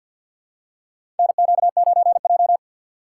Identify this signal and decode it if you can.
Morse code, heard as a keyed tone, about 52 wpm, 700 Hz. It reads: NQ9J
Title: USA Call Signs - 4 Characters - 3x Character Spacing - 50wpm